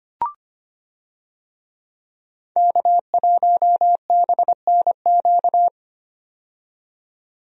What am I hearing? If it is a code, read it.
Morse code, heard as a keyed tone, about 25 wpm, 700 Hz. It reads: K1BNQ
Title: USA Call Signs - 5 Characters - 25wpm